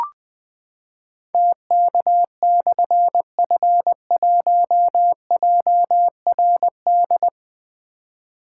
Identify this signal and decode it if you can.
Morse code, heard as a keyed tone, about 20 wpm, 700 Hz. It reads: TK/F1JRD
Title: Difficult Call Signs 20wpm